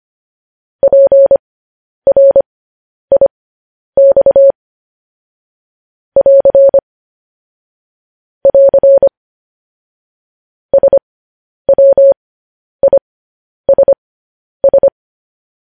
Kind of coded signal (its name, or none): Morse code